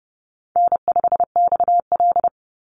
Morse code, heard as a keyed tone, about 30 wpm, 700 Hz. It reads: N5XL
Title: NA SOTA Chaser Callsigns 30wpm